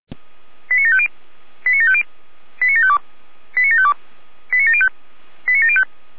Five-tone selective call (ZVEI), heard as quick runs of five tones, tones of about 70 ms. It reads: 89750 89750 89752 89752 89705 89705